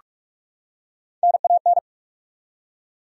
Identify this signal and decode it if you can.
Morse code, heard as a keyed tone, about 43 wpm, 700 Hz. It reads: NAN
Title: USA Common Names - Max 3 Letters - 45wpm